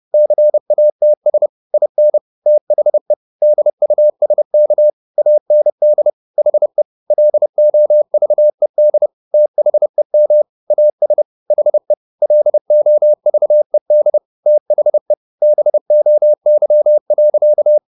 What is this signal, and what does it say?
Morse code, heard as a keyed tone, about 30 wpm, 600 Hz. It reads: CATS IN THE DUSK AND HE LOVED THEM AS HE LOVED THE BOY.
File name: the old man and the sea / the old man and the sea 1 30wpm